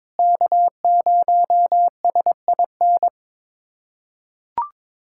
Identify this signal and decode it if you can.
Morse code, heard as a keyed tone, about 22 wpm, 700 Hz. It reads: K0SIN
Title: USA Callsign DE Callsign - 5 Characters - 22wpm